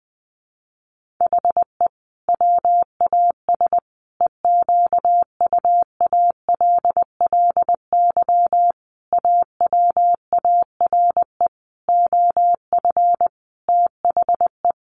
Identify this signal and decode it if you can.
Morse code, heard as a keyed tone, about 20 wpm, 700 Hz. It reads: HE WAS EQUALLY AWARE OF THE
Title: Sentences from Top 1400 Words 20wpm